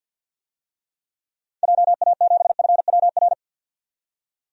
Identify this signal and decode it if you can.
Morse code, heard as a keyed tone, about 53 wpm, 700 Hz. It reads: JA7FWR
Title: DXCC-100 Call Signs - 6 Characters - 50wpm